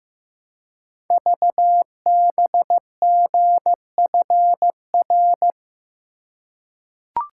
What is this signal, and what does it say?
Morse code, heard as a keyed tone, about 15 wpm, 700 Hz. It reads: VBGFR
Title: Code Groups - 5 Letters - 15wpm